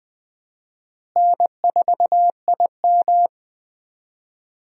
Morse code, heard as a keyed tone, about 20 wpm, 700 Hz. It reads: N4IM
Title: USA Call Signs - 4 Characters - Repeated 2x - 20wpm